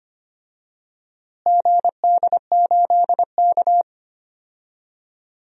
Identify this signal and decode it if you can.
Morse code, heard as a keyed tone, about 25 wpm, 700 Hz. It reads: GD8K
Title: DXCC-100 Call Signs - 4 Characters - Repeated 3x - 25wpm